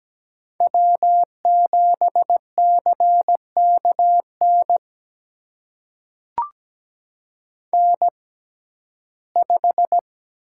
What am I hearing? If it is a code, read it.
Morse code, heard as a keyed tone, about 17 wpm, spaced as for 12 wpm, 700 Hz. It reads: W7CKN N5
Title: USA Call Signs - 5 Characters - 6x Character Spacing - 17wpm